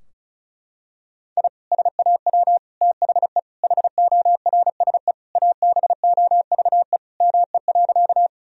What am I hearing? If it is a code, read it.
Morse code, heard as a keyed tone, about 36 wpm, 700 Hz. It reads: I SAW THE HORSE ABOVE ME.
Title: Sentences from Top 1000 Words - Review Part 2 - 35wpm